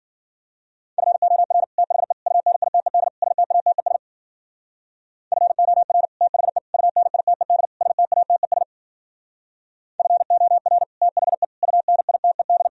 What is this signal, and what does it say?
Morse code, heard as a keyed tone, about 51 wpm, 700 Hz. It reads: FOR THE UNITED STATES FOR THE UNITED STATES FOR THE UNITED